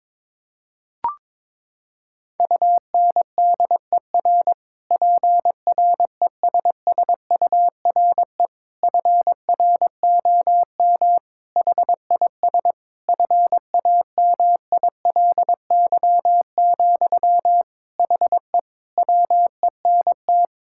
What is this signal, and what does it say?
Morse code, heard as a keyed tone, about 22 wpm, 700 Hz. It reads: UNDER PRESSURE FROM HIS FAMILY, HE WENT
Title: Sentences from Top 1000 Words - Review Part 6 - 22wpm